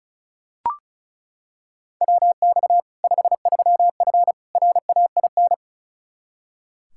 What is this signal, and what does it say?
Morse code, heard as a keyed tone, about 36 wpm, 700 Hz. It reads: WX 53F RAIN
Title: Practice QSO - Exchange 2 - Info - 35wpm